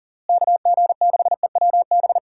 Morse code, heard as a keyed tone, about 40 wpm, 700 Hz. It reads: KG6EWB
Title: USA Call Signs - 6 Characters - Repeated 3x - 40wpm